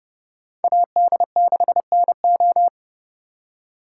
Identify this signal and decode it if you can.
Morse code, heard as a keyed tone, about 30 wpm, 700 Hz. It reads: AD6NO